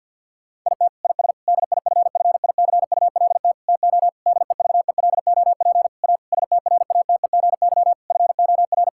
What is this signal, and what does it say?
Morse code, heard as a keyed tone, about 46 wpm, 690 Hz. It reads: IT IS DIFFICULT TO DEVELOP A STRATEGY FOR
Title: Sentences from Top 1000 Words - Review Part 5 - 50wpm